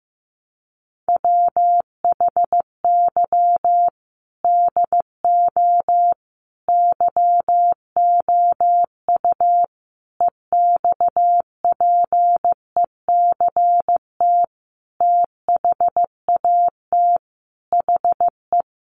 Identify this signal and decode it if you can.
Morse code, heard as a keyed tone, about 15 wpm, 700 Hz. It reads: WHY DO YOU EXPECT THAT HE
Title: Sentences from Top 1100 Words 15wpm